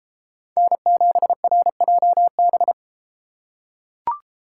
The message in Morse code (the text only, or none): N7RJB